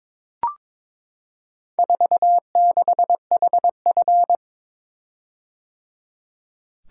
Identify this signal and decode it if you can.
Morse code, heard as a keyed tone, about 22 wpm, 700 Hz. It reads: 46HF
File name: practice / 22wpm / Four Letter-Number Combinations - 22wpm